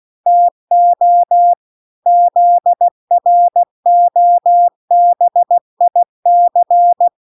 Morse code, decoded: TO ZROBIC

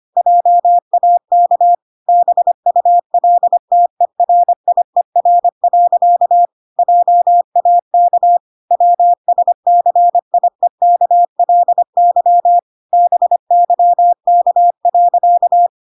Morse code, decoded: JAK BULTERIER. JAK WSCIEKLY BYK.

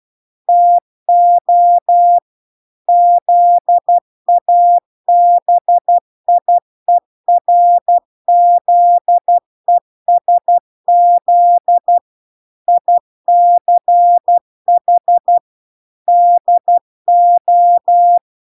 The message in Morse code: TO ZABIERZESZ ICH DO

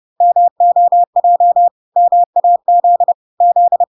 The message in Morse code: MOJ MAZ Z

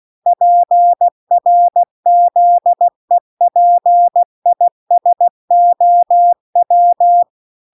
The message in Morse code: PRZEPISOW

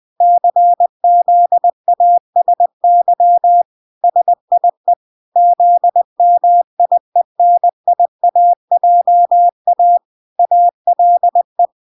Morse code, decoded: CZASY SIE ZMIENIAJA ALE

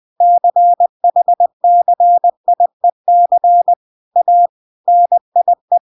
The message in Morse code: CHCIEC A NIE

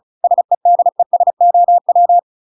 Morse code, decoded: SEDESOW

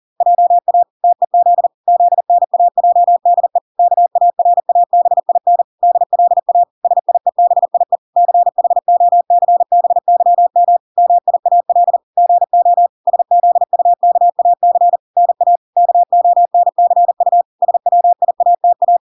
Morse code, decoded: JA TEZ ZNAJDE KARABIN DLA SIEBIE CHOCBYM MIAL GO SZUKAC NA KONCU SWIATA

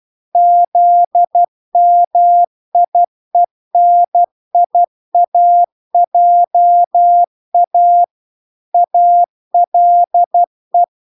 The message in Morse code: ZMIENIAJA ALE

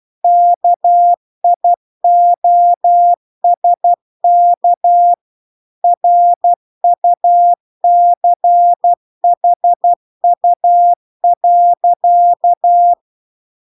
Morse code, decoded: KIOSK RUCHU.

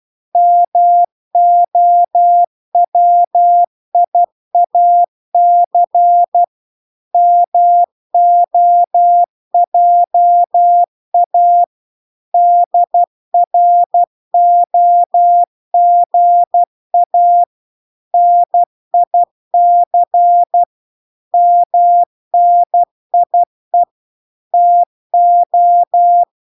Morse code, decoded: MOWIAC MOJA DROGA NIC MNIE TO

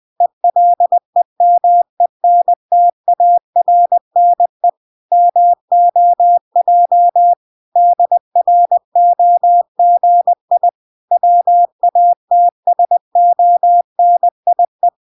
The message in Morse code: ELEMENTARNE MOJ DROGI WATSONIE

